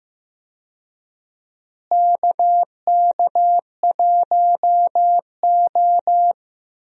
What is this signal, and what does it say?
Morse code, heard as a keyed tone, about 15 wpm, 700 Hz. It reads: KK1O
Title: USA Call Signs - 4 Characters - Repeated 3x - 15wpm